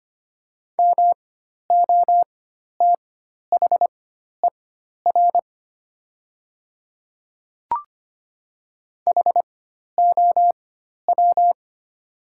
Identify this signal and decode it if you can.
Morse code, heard as a keyed tone, about 25 wpm, spaced as for 12 wpm, 690 Hz. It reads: MOTHER HOW
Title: 4x Character Spacing - Top 300 Words - 25wpm